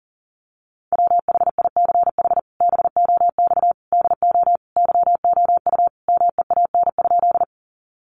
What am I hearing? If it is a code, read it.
Morse code, heard as a keyed tone, about 40 wpm, 700 Hz. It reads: WHICH BOX DO YOU MEAN?
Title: Sentences from Top 500 Words 40wpm